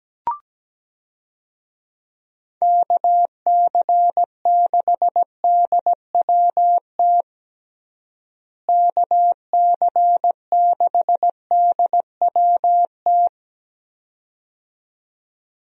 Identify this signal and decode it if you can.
Morse code, heard as a keyed tone, about 17 wpm, 700 Hz. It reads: KC6DWT KC6DWT
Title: USA Call Signs - 6 Characters - Repeated 2x - 17wpm